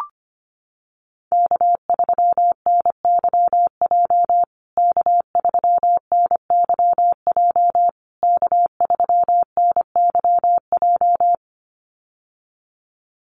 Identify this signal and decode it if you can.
Morse code, heard as a keyed tone, about 25 wpm, 700 Hz. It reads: K3NYJ K3NYJ K3NYJ
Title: USA Call Signs Repeated 3x 25wpm